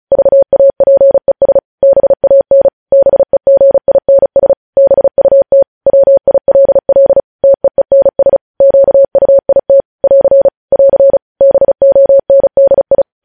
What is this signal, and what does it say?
Morse code, heard as a keyed tone, about 36 wpm, 550 Hz. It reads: VAPES BAN BEGINS BUT WILL TEENS QUIT <AR> <AR> BONDI
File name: podcast / News headlines-1x-35x35